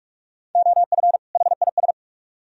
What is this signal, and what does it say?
Morse code, heard as a keyed tone, about 44 wpm, 700 Hz. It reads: OF HIS